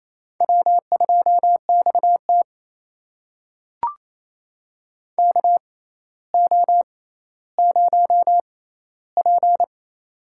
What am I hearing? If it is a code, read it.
Morse code, heard as a keyed tone, about 28 wpm, spaced as for 14 wpm, 700 Hz. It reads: W2XT KO0P
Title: USA Call Signs - 4 Characters - 6x Character Spacing - 28wpm